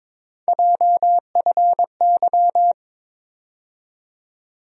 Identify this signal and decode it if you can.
Morse code, heard as a keyed tone, about 22 wpm, 700 Hz. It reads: JFY
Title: USA Call Sign Postfixes - 3 Letters 22wpm